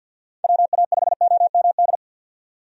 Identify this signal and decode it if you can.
Morse code, heard as a keyed tone, about 52 wpm, 700 Hz. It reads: WA5OMD
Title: USA Call Signs - 6 Characters - 50wpm